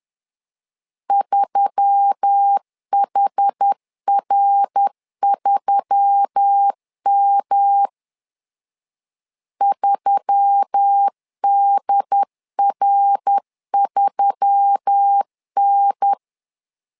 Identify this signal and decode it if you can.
Morse code, heard as a keyed tone, about 10 wpm, 800 Hz. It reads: 3HR3M 3DR3N